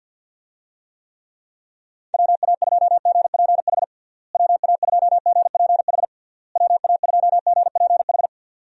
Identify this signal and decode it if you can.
Morse code, heard as a keyed tone, about 52 wpm, 700 Hz. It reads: WA2GPH WA2GPH WA2GPH
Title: USA Call Signs - 6 Characters - Repeated 3x - 50wpm